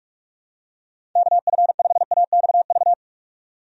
Morse code, heard as a keyed tone, about 47 wpm, 700 Hz. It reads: KF5AXV
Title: USA Call Signs - 6 Characters - 45wpm